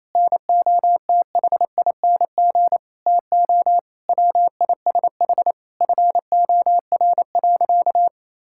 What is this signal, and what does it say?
Morse code, heard as a keyed tone, about 28 wpm, 700 Hz. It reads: NOTHING TO WISH FOR.
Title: Sentences from Top 1600 Words 28wpm